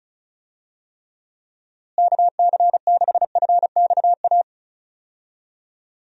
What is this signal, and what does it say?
Morse code, heard as a keyed tone, about 36 wpm, 700 Hz. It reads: KC6FXA
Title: USA Call Signs - 6 Characters - Repeated 2x - 35wpm